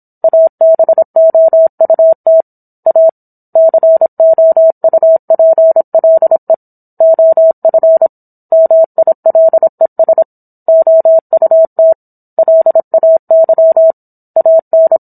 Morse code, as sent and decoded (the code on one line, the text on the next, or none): .- -... --- ..- - / .- / -.-. --- ..- .--. .-.. / --- ..-. / -- .. .-.. ... / --- ..- - / .-.. .- -.-- / .- -.
ABOUT A COUPLE OF MILES OUT LAY AN